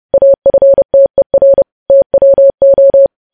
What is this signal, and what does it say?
Morse code, heard as a keyed tone, about 30 wpm, 550 Hz. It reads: AFTER TWO
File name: podcast / News headlines-1x-30x30